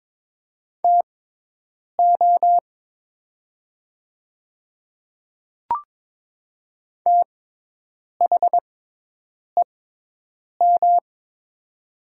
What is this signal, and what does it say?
Morse code, heard as a keyed tone, about 22 wpm, spaced as for 8 wpm, 700 Hz. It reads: TO THEM
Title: 6x Character Spacing - Top 100 Words - 22wpm